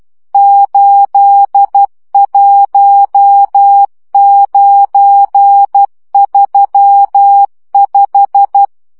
Morse code, heard as a keyed tone, about 12 wpm, 800 Hz. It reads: 81935
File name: morze vizsga 2011-3 10